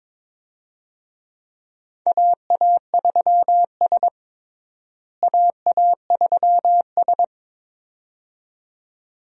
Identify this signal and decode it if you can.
Morse code, heard as a keyed tone, about 22 wpm, 700 Hz. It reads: AA3S AA3S
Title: USA Call Signs - 4 Characters - Repeated 2x - 22wpm